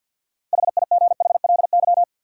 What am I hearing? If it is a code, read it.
Morse code, heard as a keyed tone, about 53 wpm, 700 Hz. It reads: HIGHLY
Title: Sentences from Top 1100 Words - New Word Drill - 50wpm